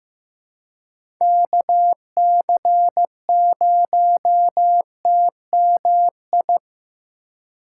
Morse code, heard as a keyed tone, about 15 wpm, 700 Hz. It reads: KC0TMI
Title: USA Call Signs - 8x Character Spacing - 15wpm